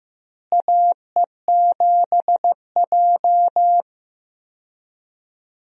Morse code, heard as a keyed tone, about 15 wpm, 700 Hz. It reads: AE7J
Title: USA Call Signs - 4 Characters - Repeated 2x - 15wpm